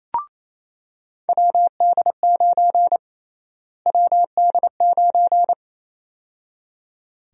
Morse code, heard as a keyed tone, about 28 wpm, 700 Hz. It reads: WD9 WD9
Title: USA Call Sign Prefixes - Repeated 2x - 28wpm